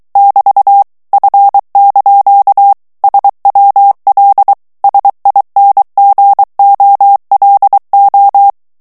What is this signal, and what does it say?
Morse code, heard as a keyed tone, about 23 wpm, 800 Hz. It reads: <BT> F) SWL SINGOLO
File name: ITALIEN texte clair 2 vitesse 22 mots minute